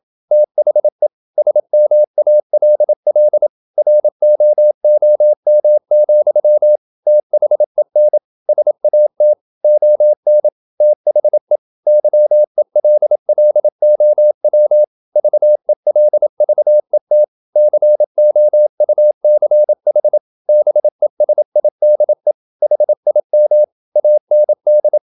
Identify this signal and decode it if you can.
Morse code, heard as a keyed tone, about 27 wpm, 600 Hz. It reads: THE SMALL ROOM, THEN SAT ON THE YELLOW VELVET COUCH BESIDE HIM AND
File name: Never 0027wpm